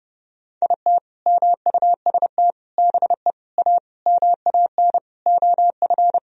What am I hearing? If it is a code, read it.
Morse code, heard as a keyed tone, about 30 wpm, 700 Hz. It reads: IT MUST BE A MAN OF
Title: Sentences from Top 1000 Words - Review Part 3 - 30wpm